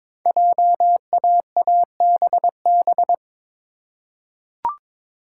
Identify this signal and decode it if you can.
Morse code, heard as a keyed tone, about 22 wpm, 700 Hz. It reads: JAABB
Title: Code Groups - 5 Letters - 22wpm